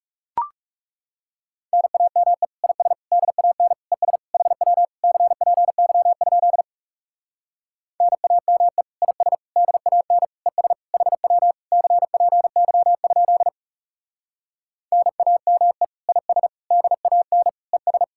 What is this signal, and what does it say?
Morse code, heard as a keyed tone, about 40 wpm, 700 Hz. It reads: NAME IS DAN ES HW CPY? NAME IS DAN ES HW CPY? NAME IS DAN ES